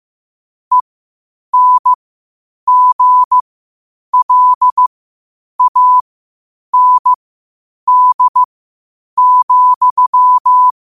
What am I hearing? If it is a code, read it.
Morse code, heard as a keyed tone, about 15 wpm, spaced as for 8 wpm, 1000 Hz. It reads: ENGLAND,
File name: qotdfiles / 8wpm